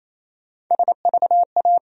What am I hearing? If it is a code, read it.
Morse code, heard as a keyed tone, about 28 wpm, 700 Hz. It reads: SVA
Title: USA Call Sign Postfixes - 3 Letters 28wpm